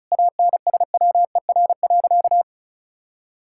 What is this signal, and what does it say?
Morse code, heard as a keyed tone, about 36 wpm, 700 Hz. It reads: ANSWER.
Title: Sentences from Top 1000 Words - Review Part 6 - 35wpm